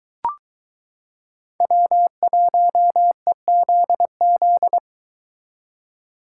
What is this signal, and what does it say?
Morse code, heard as a keyed tone, about 23 wpm, 700 Hz. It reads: W1EZZ